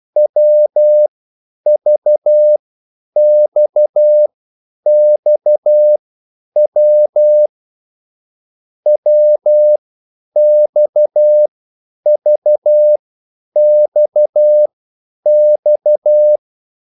Morse code, heard as a keyed tone, about 12 wpm, spaced as for 9 wpm, 600 Hz. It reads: WVXXW WXVXX
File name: QSL / CW lesson